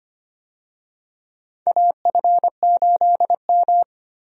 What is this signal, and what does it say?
Morse code, heard as a keyed tone, about 25 wpm, 700 Hz. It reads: AF8M
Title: USA Call Signs - 4 Characters - 25wpm